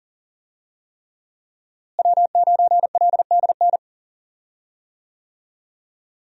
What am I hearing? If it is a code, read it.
Morse code, heard as a keyed tone, about 39 wpm, 700 Hz. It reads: W9LDN
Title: USA Call Signs - 5 Characters - Repeated 3x - 40wpm